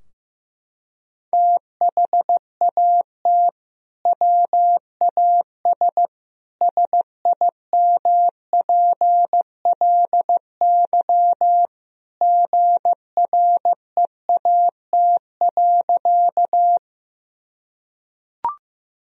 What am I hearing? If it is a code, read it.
Morse code, heard as a keyed tone, about 15 wpm, 700 Hz. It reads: THAT WAS SIMPLY GREAT.
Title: Sentences from Top 1000 Words - Review Part 4 - 15wpm